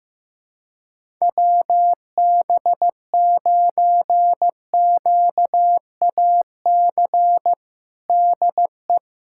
Morse code, heard as a keyed tone, about 15 wpm, 700 Hz. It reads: WB9QAC DE
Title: USA Callsign DE Callsign - 6 Characters - 15wpm